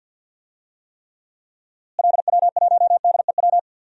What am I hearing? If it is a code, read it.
Morse code, heard as a keyed tone, about 52 wpm, 700 Hz. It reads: LW1DEW
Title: DXCC-100 Call Signs - 6 Characters - Repeated 2x - 50wpm